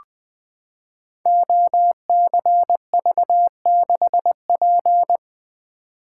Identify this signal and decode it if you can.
Morse code, heard as a keyed tone, about 20 wpm, 700 Hz. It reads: OCV6P
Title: Code Groups - 5 Letter:Numbers - 20wpm